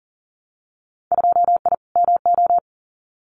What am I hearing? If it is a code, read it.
Morse code, heard as a keyed tone, about 40 wpm, 700 Hz. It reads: 2I MO
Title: Winter Field Day Exchanges 40wpm